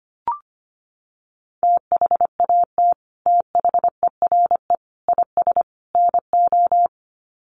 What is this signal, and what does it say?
Morse code, heard as a keyed tone, about 25 wpm, 700 Hz. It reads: THAT THERE IS NO
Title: Sets of 4 Words 25wpm